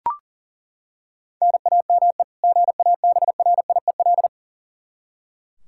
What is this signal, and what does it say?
Morse code, heard as a keyed tone, about 40 wpm, 700 Hz. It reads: NAME GABRIEL
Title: K1USN SST QSO Practice 40wpm